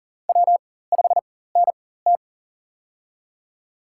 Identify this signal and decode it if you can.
Morse code, heard as a keyed tone, about 40 wpm, spaced as for 19 wpm, 700 Hz. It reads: W5NT
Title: USA Call Signs - 4 Characters - 4x Character Spacing - 40wpm